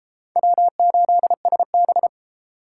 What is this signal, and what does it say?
Morse code, heard as a keyed tone, about 33 wpm, 700 Hz. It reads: W8SB